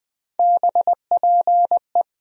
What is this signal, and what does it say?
Morse code, heard as a keyed tone, about 20 wpm, 700 Hz. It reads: BPE
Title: USA Call Sign Postfixes - 3 Letters 20wpm